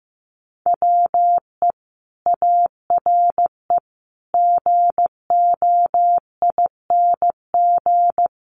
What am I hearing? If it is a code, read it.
Morse code, heard as a keyed tone, about 15 wpm, 700 Hz. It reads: WE ARE GOING
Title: Sets of 5 Words Encore 15wpm